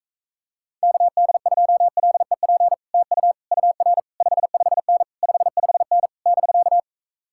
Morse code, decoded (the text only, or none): KD2LEP TU UR 55N 55N <BK>